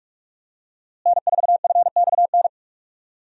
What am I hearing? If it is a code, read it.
Morse code, heard as a keyed tone, about 46 wpm, 700 Hz. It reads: N4FXN
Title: USA Call Signs - 5 Characters - Repeated 2x - 45wpm